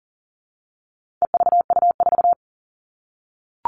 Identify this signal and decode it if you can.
Morse code, heard as a keyed tone, about 40 wpm, 700 Hz. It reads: EVU4
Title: All Cut Numbers - 4-Digit 40wpm